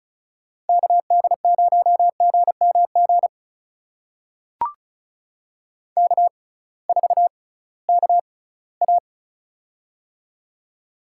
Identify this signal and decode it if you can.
Morse code, heard as a keyed tone, about 34 wpm, spaced as for 20 wpm, 700 Hz. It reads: KD0GMG K4KA
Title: USA Call Signs - 6x Character Spacing - 35wpm